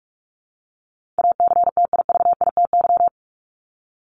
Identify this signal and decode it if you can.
Morse code, heard as a keyed tone, about 44 wpm, 700 Hz. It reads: ACTIVITY
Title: ES:S Suffix - Top 100 Words with Base Words 45wpm